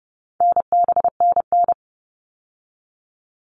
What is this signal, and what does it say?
Morse code, heard as a keyed tone, about 30 wpm, 700 Hz. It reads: NBNN
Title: All Cut Numbers - 4-Digit 30wpm